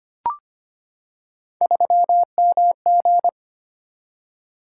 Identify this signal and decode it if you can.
Morse code, heard as a keyed tone, about 25 wpm, 700 Hz. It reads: 3MG
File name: Three Letter-Number Combinations - 25wpm